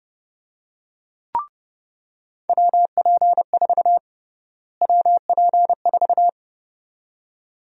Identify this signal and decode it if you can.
Morse code, heard as a keyed tone, about 30 wpm, 700 Hz. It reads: WP4 WP4
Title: USA Call Sign Prefixes - Repeated 2x - 30wpm